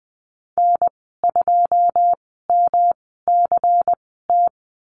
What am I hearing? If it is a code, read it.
Morse code, heard as a keyed tone, about 20 wpm, spaced as for 15 wpm, 700 Hz. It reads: N2MCT
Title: 147 Lesson 24g - Letter y - Call Signs 20wpm